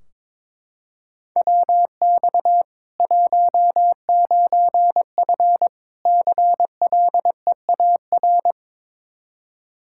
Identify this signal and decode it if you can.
Morse code, heard as a keyed tone, about 22 wpm, 700 Hz. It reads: WX 19F CLEAR
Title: Practice QSO - Exchange 2 - Info - 22wpm